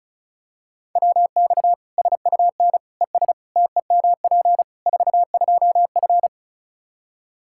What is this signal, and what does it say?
Morse code, heard as a keyed tone, about 34 wpm, 700 Hz. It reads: WX SUN ES TEMP 42F
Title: K1USN SST QSO Practice 35wpm